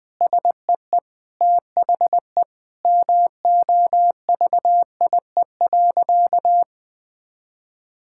Morse code, heard as a keyed tone, about 21 wpm, 700 Hz. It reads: SEE THE MOVIE.
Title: Sentences with 3-Word Phrases - Spoken Beginning - 20wpm